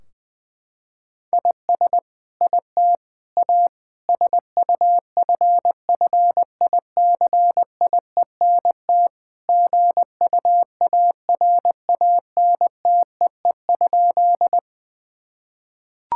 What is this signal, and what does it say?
Morse code, heard as a keyed tone, about 21 wpm, 700 Hz. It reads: IS IT A SUFFICIENT GUARANTEE?
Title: Sentences from Top 1600 Words 20wpm